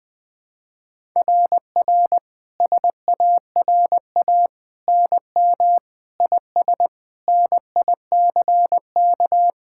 Morse code, heard as a keyed tone, about 21 wpm, 700 Hz. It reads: RR SARA NM IS NICK
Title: SOTA QSO Practice 20wpm